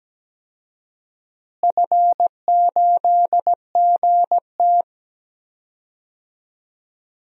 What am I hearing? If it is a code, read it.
Morse code, heard as a keyed tone, about 17 wpm, 700 Hz. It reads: F8GT
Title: DXCC-100 Call Signs - 4 Characters - Repeated 3x - 17wpm